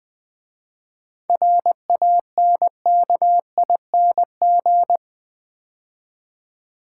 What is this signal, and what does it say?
Morse code, heard as a keyed tone, about 20 wpm, 700 Hz. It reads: RANKING
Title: 7 Letter Words - First 3 Letters Given - 20wpm